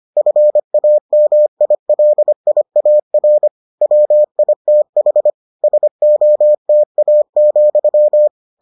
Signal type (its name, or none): Morse code